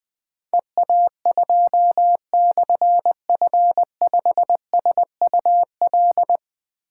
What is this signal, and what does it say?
Morse code, heard as a keyed tone, about 20 wpm, 700 Hz. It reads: EA2/F5SUL
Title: Difficult Call Signs 20wpm